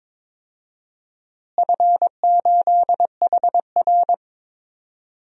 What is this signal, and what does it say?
Morse code, heard as a keyed tone, about 22 wpm, 700 Hz. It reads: F8HR
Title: DXCC-100 Call Signs - 4 Characters - Repeated 3x - 22wpm